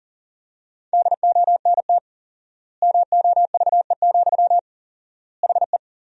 Morse code, heard as a keyed tone, about 37 wpm, 700 Hz. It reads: DONT MOVE, HE
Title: 4x Word-Spacing - Sentences from Top 300 Words - 40wpm